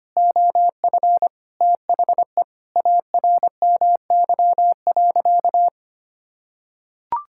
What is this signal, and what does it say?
Morse code, heard as a keyed tone, about 25 wpm, 700 Hz. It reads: OF THE ARMY.
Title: Sentences from Top 1000 Words - Review Part 6 - 25wpm